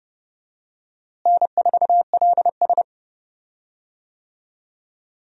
Morse code, heard as a keyed tone, about 31 wpm, 700 Hz. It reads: N4LS
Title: USA Call Signs - 4 Characters - Repeated 3x - 30wpm